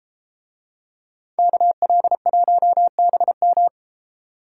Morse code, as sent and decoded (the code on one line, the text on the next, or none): -.- .-.. .---- -... --
KL1BM